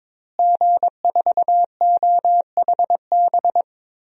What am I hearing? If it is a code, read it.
Morse code, heard as a keyed tone, about 23 wpm, 700 Hz. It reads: G4OHB
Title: DXCC-100 Call Signs - 5 Characters - Repeated 2x - 22wpm